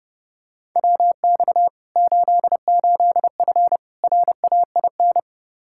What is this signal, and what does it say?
Morse code, heard as a keyed tone, about 30 wpm, 700 Hz. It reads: WX 88F RAIN
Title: Practice QSO - Exchange 2 - Info - 30wpm